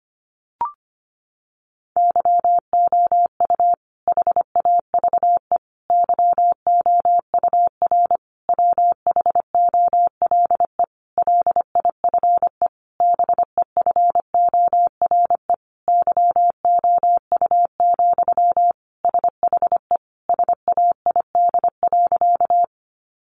Morse code, heard as a keyed tone, about 25 wpm, 700 Hz. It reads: YOU HAVE YOUR WHOLE LIFE BEFORE YOU, SHE SAID.
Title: Sentences from Top 400 Words 25wpm